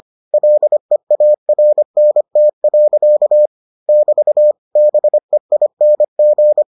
Morse code, heard as a keyed tone, about 25 wpm, 600 Hz. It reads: LEARNT. <BT> BEING